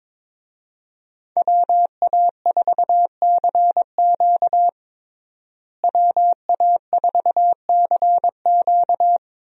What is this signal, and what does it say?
Morse code, heard as a keyed tone, about 22 wpm, 700 Hz. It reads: WA4CQ WA4CQ
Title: USA Call Signs - 5 Characters - Repeated 2x - 22wpm